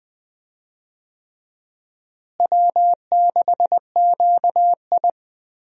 Morse code, heard as a keyed tone, about 20 wpm, 700 Hz. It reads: W6QI
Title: USA Call Signs - 4 Characters - Repeated 3x - 20wpm